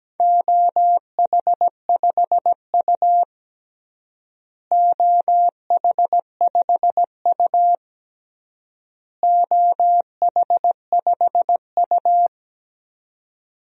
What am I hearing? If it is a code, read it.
Morse code, heard as a keyed tone, about 17 wpm, 700 Hz. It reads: OH5U OH5U OH5U
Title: DXCC-100 Call Signs - 4 Characters - Repeated 3x - 17wpm